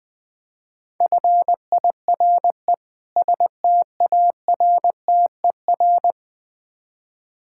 Morse code, heard as a keyed tone, about 20 wpm, 700 Hz. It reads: FIRE STARTER
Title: Difficult Words - Speed-Racing with Hints - 20wpm